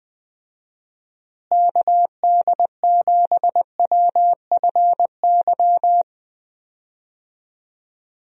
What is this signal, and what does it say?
Morse code, heard as a keyed tone, about 20 wpm, 700 Hz. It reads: KD7WFY
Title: USA Call Signs - 6 Characters - Repeated 2x - 20wpm